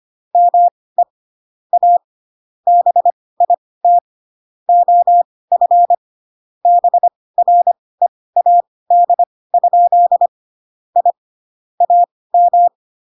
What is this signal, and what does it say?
Morse code, heard as a keyed tone, about 25 wpm, spaced as for 18 wpm, 700 Hz. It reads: ME A BIT OF BREAD? I AM
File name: Pinocchio / Pinocchio ch6 12 25 ik1hge